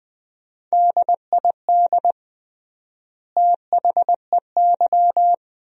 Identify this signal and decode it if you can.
Morse code, heard as a keyed tone, about 20 wpm, 700 Hz. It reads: DID THEY
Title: 3x Word-Spacing - Sentences from Top 400 Words - 20wpm